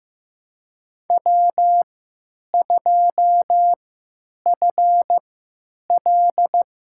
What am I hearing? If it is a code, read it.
Morse code, heard as a keyed tone, about 15 wpm, spaced as for 9 wpm, 700 Hz. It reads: W2FL
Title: USA Call Signs - 4 Characters - 3x Character Spacing - 15wpm